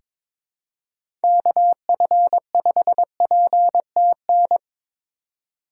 Morse code, heard as a keyed tone, about 22 wpm, 700 Hz. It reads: KF5PTN
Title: USA Call Signs - 6 Characters - 4x Character Spacing - 22wpm